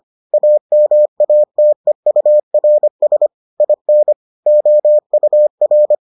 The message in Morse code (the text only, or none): AMATEURS IN OUR